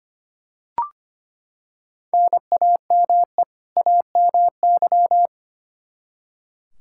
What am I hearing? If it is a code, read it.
Morse code, heard as a keyed tone, about 25 wpm, 700 Hz. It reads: NAME AMY
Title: Practice QSO - Exchange 1 - Beginning - 25wpm